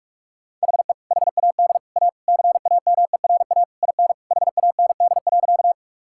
Morse code, heard as a keyed tone, about 46 wpm, 700 Hz. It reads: HE HAD A CAMERA IN HAND.
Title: Sentences from Top 1000 Words - Review Part 5 - 45wpm